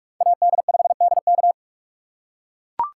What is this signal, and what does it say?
Morse code, heard as a keyed tone, about 44 wpm, 700 Hz. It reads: AD5DK